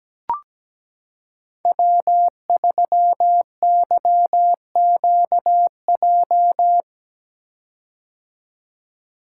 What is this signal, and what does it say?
Morse code, heard as a keyed tone, about 17 wpm, 700 Hz. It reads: W3YQJ
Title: Code Groups - 5 Letter:Numbers - 17wpm